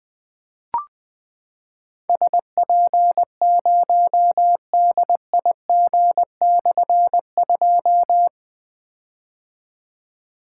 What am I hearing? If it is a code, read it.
Morse code, heard as a keyed tone, about 20 wpm, 700 Hz. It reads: SP0DIG/2
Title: Difficult Call Signs 20wpm